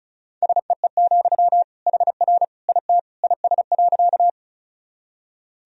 Morse code, heard as a keyed tone, about 36 wpm, 700 Hz. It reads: SEE, HR IT IS.
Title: Sentences from Top 200 Words and Abbreviations 35wpm